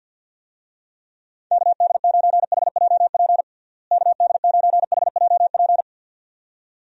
Morse code, heard as a keyed tone, about 52 wpm, 700 Hz. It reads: KD9HJP KD9HJP
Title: USA Call Signs - 6 Characters - Repeated 2x - 50wpm